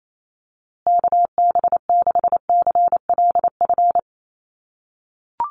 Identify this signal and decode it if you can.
Morse code, heard as a keyed tone, about 28 wpm, 690 Hz. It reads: KB6CLF
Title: USA Call Signs Repeated 3x 28wpm